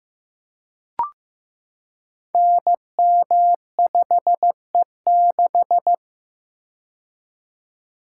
Morse code, heard as a keyed tone, about 15 wpm, 700 Hz. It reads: NM5E6
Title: Code Groups - 5 Letter:Numbers - 15wpm